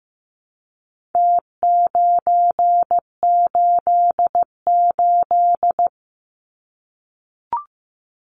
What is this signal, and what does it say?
Morse code, heard as a keyed tone, about 15 wpm, 700 Hz. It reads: T988
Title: Numbers - Contest Style - 4-Digit 15wpm